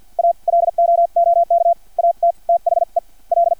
Morse code, heard as a keyed tone, about 50 wpm, spaced as for 43 wpm, 680 Hz. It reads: A LOOK AT THE L
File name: archived / CRSnet